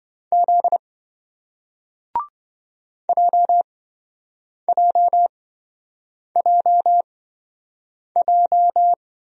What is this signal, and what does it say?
Morse code, heard as a keyed tone, about 27 wpm, spaced as for 7 wpm, 700 Hz. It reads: Z JJJJ